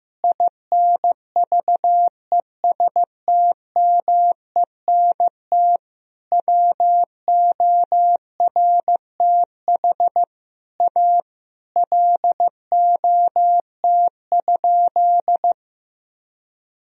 Morse code, heard as a keyed tone, about 15 wpm, 700 Hz. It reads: INVESTMENT WORTH A LOT?
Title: Sentences from Top 1000 Words - Review Part 4 - 15wpm